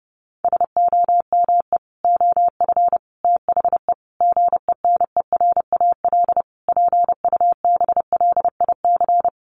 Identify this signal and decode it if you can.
Morse code, heard as a keyed tone, about 30 wpm, 700 Hz. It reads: SOME OF THE GENERAL PUBLIC
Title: Sentences from Top 800 Words 30wpm